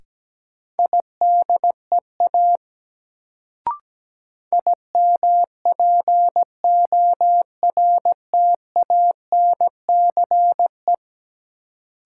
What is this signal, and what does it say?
Morse code, heard as a keyed tone, about 17 wpm, 700 Hz. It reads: IDEA IMPORTANCE
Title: Sets of 3 Words - New Word Drill - 17wpm